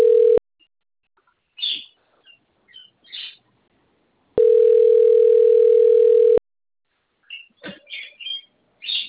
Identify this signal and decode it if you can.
Ringback tone (North America), 440 + 480 Hz.